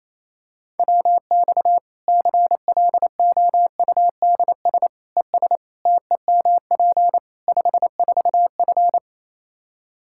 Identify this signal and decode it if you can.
Morse code, heard as a keyed tone, about 28 wpm, 700 Hz. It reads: WX CLOUDS ES TEMP 54F
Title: Practice QSO - Exchange 2 - Info - 28wpm